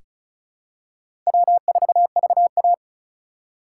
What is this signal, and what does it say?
Morse code, heard as a keyed tone, about 36 wpm, 700 Hz. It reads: W4VA
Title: USA Call Signs - 4 Characters - 4x Character Spacing - 35wpm